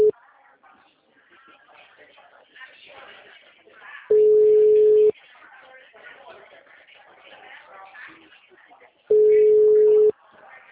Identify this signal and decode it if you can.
Ringback tone (Europe, 425 Hz), with speech.